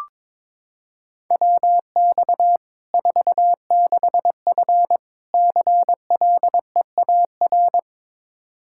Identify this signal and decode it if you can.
Morse code, heard as a keyed tone, about 22 wpm, 700 Hz. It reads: WX 46F CLEAR
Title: Practice QSO - Exchange 2 - Info - 22wpm